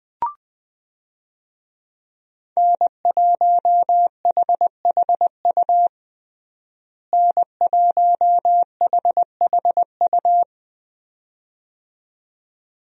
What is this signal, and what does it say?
Morse code, heard as a keyed tone, about 20 wpm, 700 Hz. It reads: N1HHU N1HHU
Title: USA Call Signs - 5 Characters - Repeated 2x - 20wpm